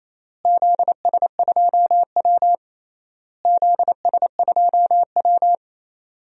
Morse code, heard as a keyed tone, about 27 wpm, 700 Hz. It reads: ZS2W ZS2W
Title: DXCC-100 Call Signs - 4 Characters - Repeated 3x - 28wpm